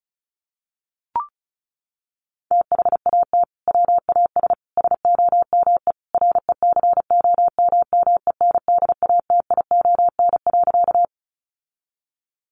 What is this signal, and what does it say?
Morse code, heard as a keyed tone, about 34 wpm, 700 Hz. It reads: THAT WAS SOME RECOMMENDATION.
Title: Sentences from Top 1400 Words 35wpm